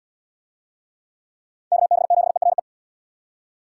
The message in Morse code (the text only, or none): KB7ERE